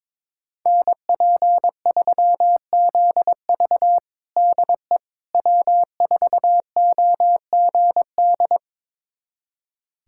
Morse code, heard as a keyed tone, about 22 wpm, 700 Hz. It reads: NP3ZV DE W4OGD
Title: USA Callsign DE Callsign - 5 Characters - 22wpm